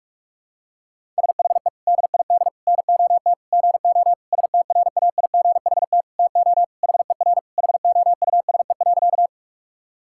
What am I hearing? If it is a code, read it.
Morse code, heard as a keyed tone, about 46 wpm, 700 Hz. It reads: SHE DID NOT GO STRAIGHT TO HER HOUSE.
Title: Sentences from Top 1000 Words - Review Part 6 - 45wpm